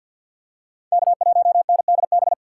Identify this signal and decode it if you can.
Morse code, heard as a keyed tone, about 53 wpm, 700 Hz. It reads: K1NDB
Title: USA Call Signs - 5 Characters - 6x Character Spacing - 50wpm